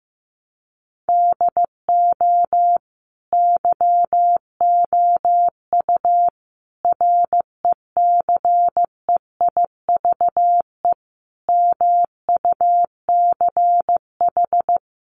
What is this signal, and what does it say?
Morse code, heard as a keyed tone, about 15 wpm, 700 Hz. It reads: DO YOU RECEIVE MUCH
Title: Sentences from Top 1000 Words 15wpm